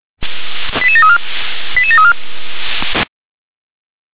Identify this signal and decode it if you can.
Five-tone selective call (ZVEI), heard as quick runs of five tones, tones of about 80 ms. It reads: 88734 88734